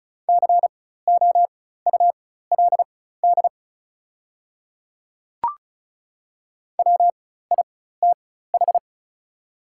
Morse code, heard as a keyed tone, about 36 wpm, spaced as for 16 wpm, 690 Hz. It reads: COULD WITH
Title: 4x Character Spacing - Top 100 Words - 35wpm